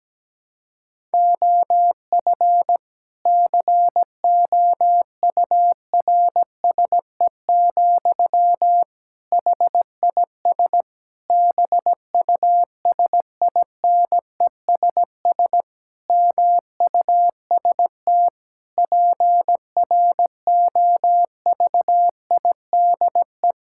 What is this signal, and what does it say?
Morse code, heard as a keyed tone, about 17 wpm, 700 Hz. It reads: OF COURSE, HIS BUSINESS MUST PROVIDE
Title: Sentences from Top 1000 Words - Review Part 5 - 17wpm